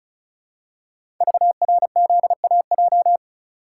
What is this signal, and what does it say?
Morse code, heard as a keyed tone, about 36 wpm, 700 Hz. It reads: VRZAJ
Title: Code Groups - 5 Letters - 35wpm